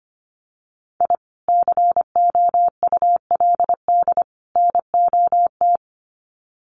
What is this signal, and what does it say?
Morse code, heard as a keyed tone, about 25 wpm, 700 Hz. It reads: I COULD NOT
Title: Sets of 3 Words 25wpm